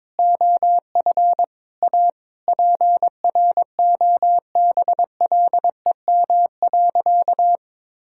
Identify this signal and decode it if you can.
Morse code, heard as a keyed tone, about 22 wpm, 700 Hz. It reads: OF A PROBLEM.
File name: practice / 22wpm / Sentences with 8-Word Phrases - 22wpm